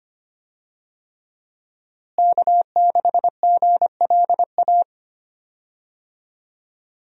Morse code, heard as a keyed tone, about 25 wpm, 700 Hz. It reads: K6GLA
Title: USA Call Signs - 5 Characters - Repeated 3x - 25wpm